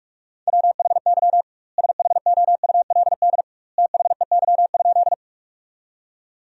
Morse code, heard as a keyed tone, about 46 wpm, 700 Hz. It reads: WHY SHOULD THEY?
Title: 2x Word-Spacing - Sentences from Top 300 Words - 45wpm